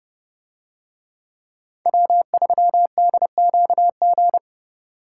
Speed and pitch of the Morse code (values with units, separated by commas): 30 wpm, 700 Hz